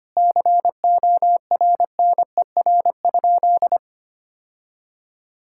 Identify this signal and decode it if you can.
Morse code, heard as a keyed tone, about 25 wpm, 700 Hz. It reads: CORNER?
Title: Sentences from Top 1600 Words 25wpm